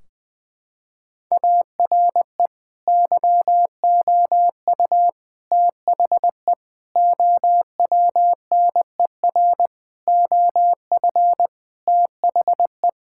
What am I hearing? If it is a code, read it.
Morse code, heard as a keyed tone, about 20 wpm, 700 Hz. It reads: ARE YOU THE OWNER OF THE
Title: Sentences from Top 1600 Words 20wpm